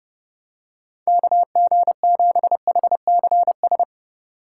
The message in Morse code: KG7HCS

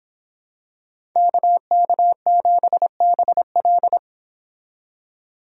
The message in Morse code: KK7BL

